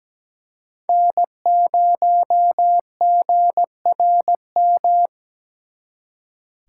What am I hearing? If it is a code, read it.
Morse code, heard as a keyed tone, about 20 wpm, 700 Hz. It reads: N0GRM
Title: Speed Racing - USA 1x3 Call Signs 17wpm